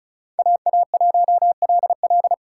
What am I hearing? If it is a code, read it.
Morse code, heard as a keyed tone, about 34 wpm, 700 Hz. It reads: AA1LL
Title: USA Call Signs - 5 Characters - 3x Character Spacing - 35wpm